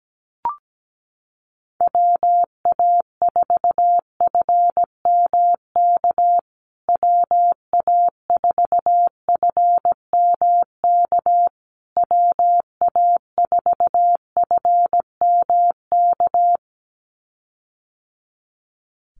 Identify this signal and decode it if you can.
Morse code, heard as a keyed tone, about 17 wpm, 700 Hz. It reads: WA4FMK WA4FMK WA4FMK
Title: USA Call Signs Repeated 3x 17wpm